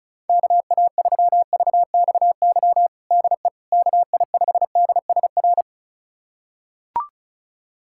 Morse code, heard as a keyed tone, about 36 wpm, 700 Hz. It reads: KA3VXY DE KI5DSR
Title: USA Callsign DE Callsign 35wpm